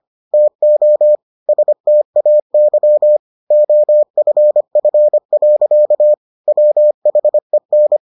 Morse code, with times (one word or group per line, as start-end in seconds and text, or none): 0.3-1.2 TO
1.5-3.2 STAY
3.5-6.1 OFF.
6.5-8.0 WHEN